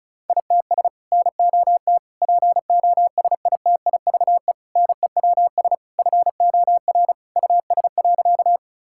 Morse code, heard as a keyed tone, about 36 wpm, 700 Hz. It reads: ITS NOT POSITIVE NEWS FOR US.
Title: Sentences from Top 1000 Words - Review Part 5 - 35wpm